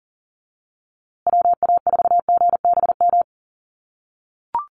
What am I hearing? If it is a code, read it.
Morse code, heard as a keyed tone, about 40 wpm, 700 Hz. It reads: WA4GBM
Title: USA Call Signs Repeated 3x 40wpm